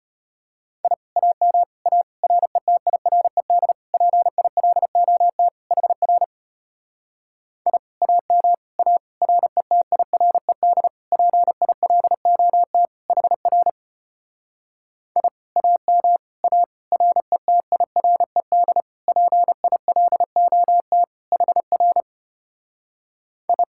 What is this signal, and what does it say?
Morse code, heard as a keyed tone, about 33 wpm, 700 Hz. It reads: I AM A RETIRED PILOT HR I AM A RETIRED PILOT HR I AM A RETIRED PILOT HR I